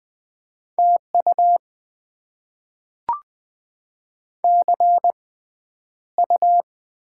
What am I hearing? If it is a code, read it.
Morse code, heard as a keyed tone, about 20 wpm, spaced as for 10 wpm, 700 Hz. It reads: TU CU
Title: Top 20 QSO Elements - 6x Character Spacing - 20wpm